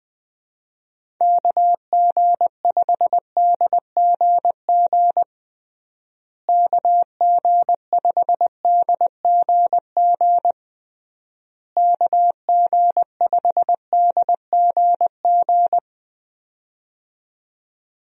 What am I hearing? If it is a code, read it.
Morse code, heard as a keyed tone, about 20 wpm, 700 Hz. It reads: KG5DGG KG5DGG KG5DGG
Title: USA Call Signs - 6 Characters - Repeated 3x - 20wpm